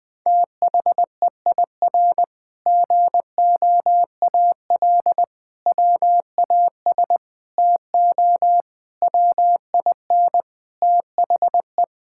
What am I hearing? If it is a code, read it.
Morse code, heard as a keyed tone, about 20 wpm, 700 Hz. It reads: THEIR GOAL WAS TO WIN THE
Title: Sentences with 8-Word Phrases - Spoken Beginning - 20wpm